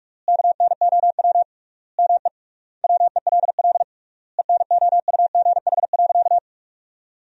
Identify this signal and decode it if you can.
Morse code, heard as a keyed tone, about 48 wpm, 700 Hz. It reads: KNOW ME WELL ENOUGH.
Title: 3x Word-Spacing - Sentences from Top 400 Words - 45wpm